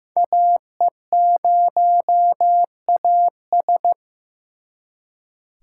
Morse code, heard as a keyed tone, about 18 wpm, 700 Hz. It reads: AE0AS
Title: Speed Racing - USA 2x2 Call Signs 15wpm